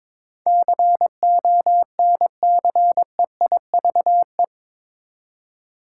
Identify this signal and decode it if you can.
Morse code, heard as a keyed tone, about 22 wpm, 700 Hz. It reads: CONCEIVE
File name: practice / 22wpm / Eight Letter Words - 22wpm